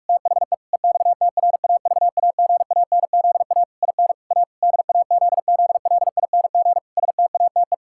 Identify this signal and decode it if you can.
Morse code, heard as a keyed tone, about 43 wpm, 700 Hz. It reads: THE EXTRAVAGANZA IN A DAZZLING STATE
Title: Pangram Challenge v2 - 45wpm